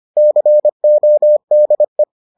Morse code, heard as a keyed tone, about 25 wpm, 600 Hz. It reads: CODE